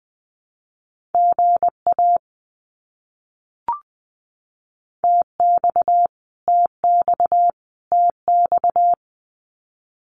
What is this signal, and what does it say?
Morse code, heard as a keyed tone, about 20 wpm, 700 Hz. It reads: GA TX TX TX
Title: Abbreviations Repeated 3x - 20wpm